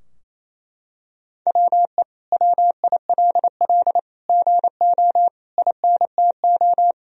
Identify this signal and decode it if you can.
Morse code, heard as a keyed tone, about 28 wpm, 700 Hz. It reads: WE WILL GO INTO
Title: Sentences from Top 1000 Words - Review Part 4 - 28wpm